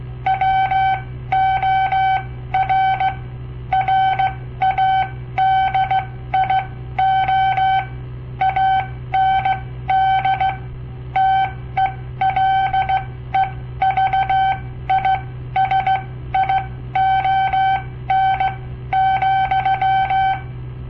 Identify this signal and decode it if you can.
Morse code, heard as a keyed tone, about 16 wpm, spaced as for 13 wpm, 770 Hz. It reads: WOR RADIO AND TELEVISION,
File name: papertapes / local / Code Practice-01